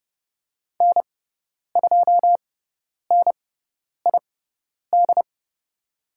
Morse code, heard as a keyed tone, about 30 wpm, spaced as for 10 wpm, 700 Hz. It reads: N2NID
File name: Lesson 18g - Letter m - Call Signs 30wpm